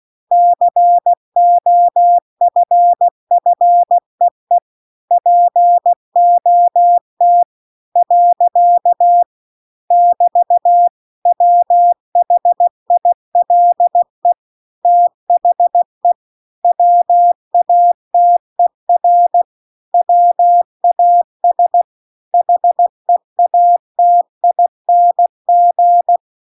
Morse code, heard as a keyed tone, about 16 wpm, 700 Hz. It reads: COFFEE POT . <BT> WHILE THE WATER WAS HEATING
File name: CW-The-Big-Twoheaded-River-2-16WPM